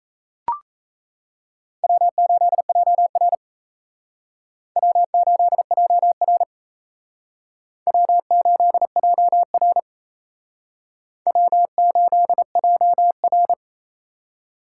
Morse code, heard as a keyed tone, about 37 wpm, 700 Hz. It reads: W8JR W8JR W8JR W8JR